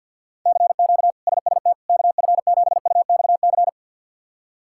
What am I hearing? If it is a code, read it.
Morse code, heard as a keyed tone, about 52 wpm, 700 Hz. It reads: CQ SST KF7UXC
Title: K1USN SST QSO Practice - 50wpm